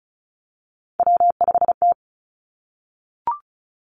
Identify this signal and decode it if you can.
Morse code, heard as a keyed tone, about 36 wpm, 700 Hz. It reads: W5T
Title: Three Challenging Letter-Number Combinations 35wpm